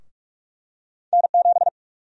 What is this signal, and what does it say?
Morse code, heard as a keyed tone, about 46 wpm, 700 Hz. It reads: N7
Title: USA Call Sign Prefixes - Repeated 2x - 45wpm